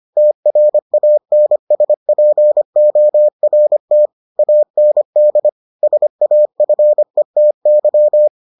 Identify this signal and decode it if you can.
Morse code, heard as a keyed tone, about 25 wpm, 600 Hz. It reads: TRANSPORT AND SAFETY